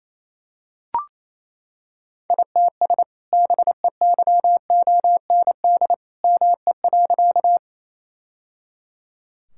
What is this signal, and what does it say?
Morse code, heard as a keyed tone, about 28 wpm, 700 Hz. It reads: ITS BEYOND ME.
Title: Sentences from Top 1000 Words - Review Part 6 - 28wpm